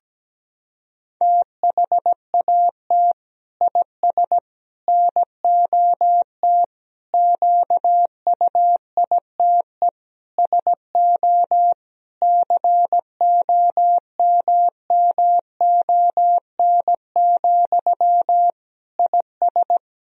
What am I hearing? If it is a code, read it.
Morse code, heard as a keyed tone, about 17 wpm, 700 Hz. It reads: THAT IS NOT QUITE SO COMMON, IS
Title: Sentences from Top 1000 Words - Review Part 3 - 17wpm